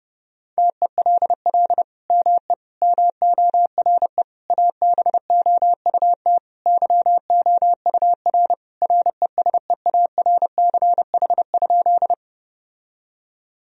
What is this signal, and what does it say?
Morse code, heard as a keyed tone, about 30 wpm, 700 Hz. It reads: TELL ME MORE ABOUT YOUR RESEARCH?
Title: Sentences from Top 1000 Words - Review Part 4 - 30wpm